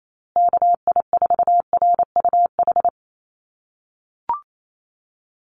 Morse code, heard as a keyed tone, about 28 wpm, 700 Hz. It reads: KI4RUH
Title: USA Call Signs Repeated 2x 28wpm